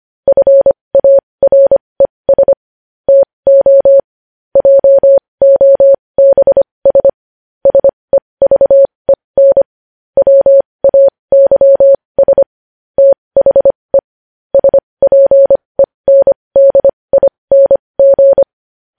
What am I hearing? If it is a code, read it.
Morse code, heard as a keyed tone, about 25 wpm, spaced as for 20 wpm, 550 Hz. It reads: FARES TO JOBS SEVEN WAYS THE SPENDING
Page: News Headlines in Morse Code at 20 WPM